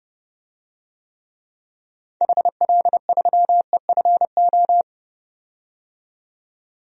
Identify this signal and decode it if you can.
Morse code, heard as a keyed tone, about 32 wpm, 700 Hz. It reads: HL3EFO
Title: DXCC-100 Call Signs - 6 Characters - 30wpm